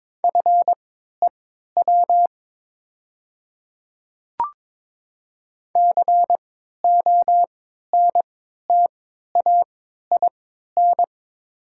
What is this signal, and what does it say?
Morse code, heard as a keyed tone, about 22 wpm, spaced as for 13 wpm, 700 Hz. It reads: FEW CONTAIN
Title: 3x Character Spacing - Top 500 Words - 22wpm